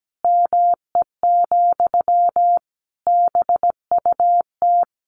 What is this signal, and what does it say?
Morse code, heard as a keyed tone, about 17 wpm, 700 Hz. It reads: ME, BUT
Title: Sentences from Top 1500 Words 17wpm